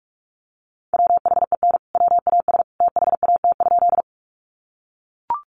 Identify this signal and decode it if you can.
Morse code, heard as a keyed tone, about 44 wpm, 700 Hz. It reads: WHEN WAS THAT?
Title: Sentences from Top 100 Words 45wpm